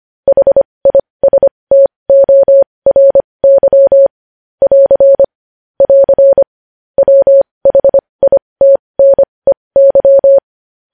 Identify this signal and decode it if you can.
Morse code, heard as a keyed tone, about 25 wpm, spaced as for 20 wpm, 550 Hz. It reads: HISTORY <AR> <AR> WHITNEY
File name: podcast / News headlines-1x-25x20